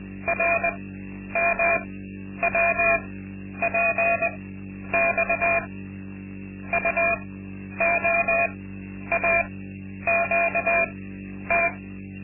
Morse code, heard as a keyed tone, about 20 wpm, spaced as for 10 wpm, 680 Hz, with noise. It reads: RMWPX UOAQT
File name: Linksmakalnis-2